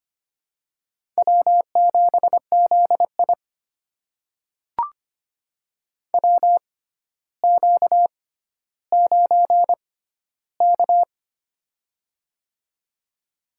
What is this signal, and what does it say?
Morse code, heard as a keyed tone, about 25 wpm, spaced as for 13 wpm, 700 Hz. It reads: W7ZI WQ9K
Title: USA Call Signs - 4 Characters - 6x Character Spacing - 25wpm